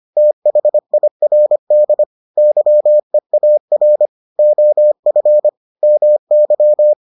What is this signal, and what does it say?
Morse code, heard as a keyed tone, about 25 wpm, 600 Hz. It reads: THIRD YEAR OF MY